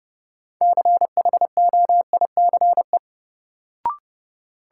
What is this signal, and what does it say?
Morse code, heard as a keyed tone, about 30 wpm, 700 Hz. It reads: CHOICE
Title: Sets of 5 Words - New Word Drill - 30wpm